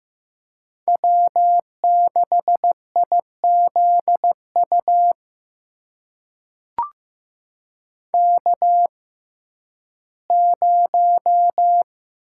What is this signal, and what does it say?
Morse code, heard as a keyed tone, about 15 wpm, spaced as for 11 wpm, 700 Hz. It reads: W6IZU K0
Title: USA Call Signs - 6x Character Spacing - 15wpm